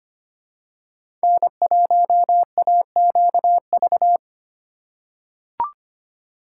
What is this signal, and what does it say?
Morse code, heard as a keyed tone, about 25 wpm, 700 Hz. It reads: N1AQV
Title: USA Call Signs - 5 Characters - 4x Character Spacing - 25wpm